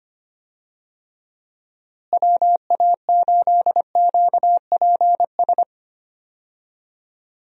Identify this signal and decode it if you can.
Morse code, heard as a keyed tone, about 25 wpm, 700 Hz. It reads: WA8QPS
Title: USA Call Signs - 6 Characters - Repeated 3x - 25wpm